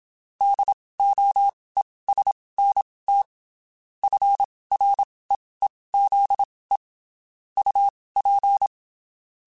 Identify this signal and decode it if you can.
Morse code, heard as a keyed tone, about 27 wpm, spaced as for 19 wpm, 800 Hz. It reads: DOESNT FREEZE UP
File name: (spaced) / QSOs Spaced-mp3s / WD0FBT @25 sp